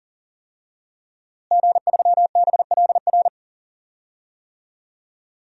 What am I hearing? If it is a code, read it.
Morse code, heard as a keyed tone, about 40 wpm, 700 Hz. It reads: G3BLR
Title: DXCC-100 Call Signs - 5 Characters - Repeated 3x - 40wpm